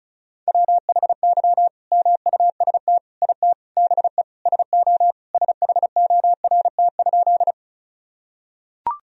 Morse code, heard as a keyed tone, about 36 wpm, 700 Hz. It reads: WHY MUST IT BE SO SHORT?
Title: Sentences from Top 1000 Words - Review Part 2 - 35wpm